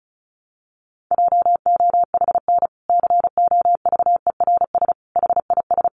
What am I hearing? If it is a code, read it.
Morse code, heard as a keyed tone, about 34 wpm, 700 Hz. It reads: JOHN COVERS HIS
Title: Easy Phrases 35wpm